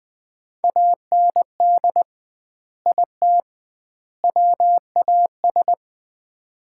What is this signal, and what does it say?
Morse code, heard as a keyed tone, about 20 wpm, 700 Hz. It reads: AND IT WAS
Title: 2x Word-Spacing - Sets of 3 Words - 20wpm